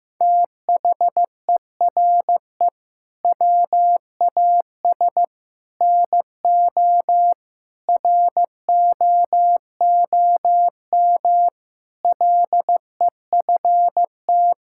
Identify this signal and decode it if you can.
Morse code, heard as a keyed tone, about 15 wpm, 700 Hz. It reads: THERE WAS NO ROOM LEFT
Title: Sentences from Top 1600 Words 15wpm